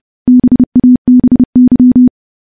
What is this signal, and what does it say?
Morse code, heard as a keyed tone, about 30 wpm, 250 Hz. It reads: BABY
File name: morse / morse-training-30WPM